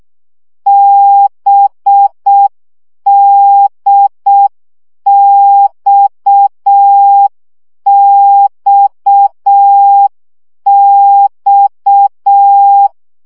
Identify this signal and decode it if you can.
Morse code, heard as a keyed tone, about 6 wpm, 800 Hz. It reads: BDXXX
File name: CW gyakorlas d b x 5 baud